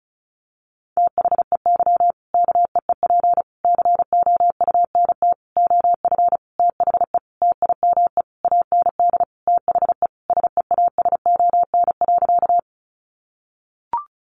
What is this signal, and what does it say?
Morse code, heard as a keyed tone, about 34 wpm, 700 Hz. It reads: THEY KEEP COUNT OF THE TIME AND THE SEASON.
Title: Sentences from Top 1500 Words 35wpm